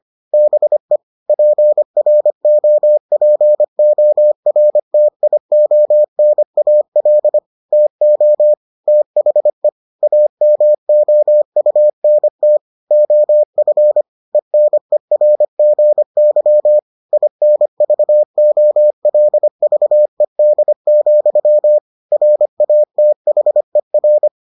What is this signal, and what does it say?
Morse code, heard as a keyed tone, about 25 wpm, 600 Hz. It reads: BE PROPORTIONAL TO THE AMOUNT OF ENERGY INVOLVED, RATHER